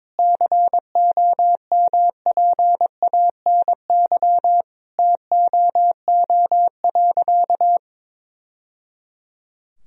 Morse code, heard as a keyed tone, about 21 wpm, 700 Hz. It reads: COMPANY TOO.
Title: Sentences from Top 1000 Words - Review Part 3 - 22wpm